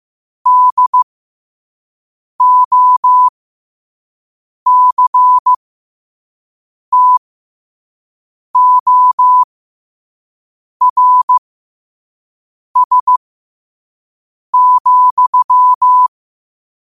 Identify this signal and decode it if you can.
Morse code, heard as a keyed tone, about 15 wpm, spaced as for 5 wpm, 1000 Hz. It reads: DOCTORS,
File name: qotdfiles / 5wpm